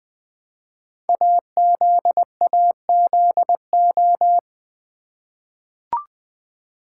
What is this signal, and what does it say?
Morse code, heard as a keyed tone, about 20 wpm, 700 Hz. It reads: AZAZO
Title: Code Groups - 5 Letters - 20wpm